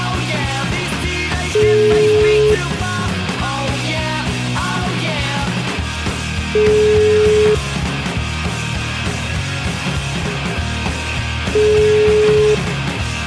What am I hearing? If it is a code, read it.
Ringback tone (Europe, 425 Hz), with noise.